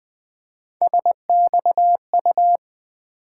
Morse code, heard as a keyed tone, about 20 wpm, 700 Hz. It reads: SXU
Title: Three Letter-Number Combinations - 20wpm